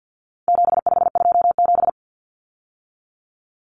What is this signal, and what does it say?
Morse code, heard as a keyed tone, about 50 wpm, 700 Hz. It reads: 7527
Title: Numbers - 4-Digit 50wpm